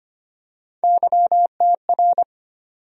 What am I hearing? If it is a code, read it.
Morse code, heard as a keyed tone, about 25 wpm, 700 Hz. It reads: YTR
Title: USA Call Sign Postfixes - 3 Letters 25wpm